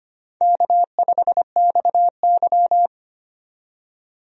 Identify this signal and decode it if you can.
Morse code, heard as a keyed tone, about 25 wpm, 700 Hz. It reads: K5XY
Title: USA Call Signs - 4 Characters - 25wpm